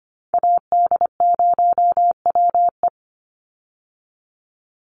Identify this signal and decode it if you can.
Morse code, heard as a keyed tone, about 25 wpm, 700 Hz. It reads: AD0WE
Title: Call Signs Repeated 2x - 25wpm